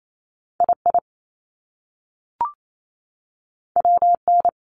Morse code, heard as a keyed tone, about 28 wpm, 700 Hz. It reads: II WN
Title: Top Two Letter Combinations in Words 28wpm